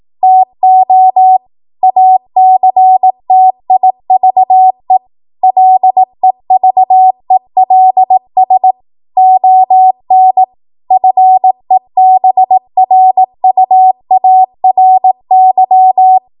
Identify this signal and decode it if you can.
Morse code, heard as a keyed tone, about 18 wpm, 750 Hz. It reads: TO ACTIVE LEVELS ON FEBRUARY
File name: Morse / 18WPM